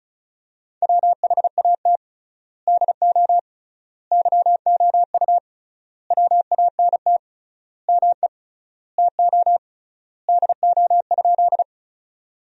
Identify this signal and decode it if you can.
Morse code, heard as a keyed tone, about 34 wpm, 700 Hz. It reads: WHAT DO YOU WANT ME TO DO?
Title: 3x Word-Spacing - Sentences from Top 200 Words - 35wpm